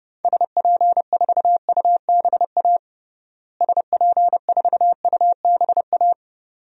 Morse code, heard as a keyed tone, about 30 wpm, 700 Hz. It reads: SP4UBA SP4UBA
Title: DXCC-100 Call Signs - 6 Characters - Repeated 2x - 30wpm